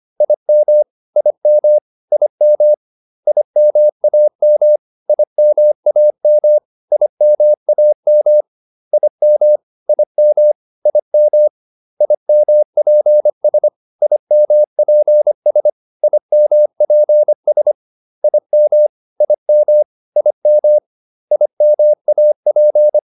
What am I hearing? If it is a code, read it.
Morse code, heard as a keyed tone, about 25 wpm, 600 Hz. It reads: IM IM IM IMAM IMAM IMAM IM IM IM IMPS IMPS IMPS IM IM IM IMAP